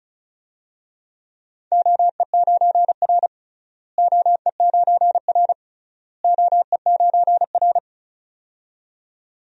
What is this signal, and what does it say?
Morse code, heard as a keyed tone, about 36 wpm, 700 Hz. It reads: OE9R OE9R OE9R
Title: DXCC-100 Call Signs - 4 Characters - Repeated 3x - 35wpm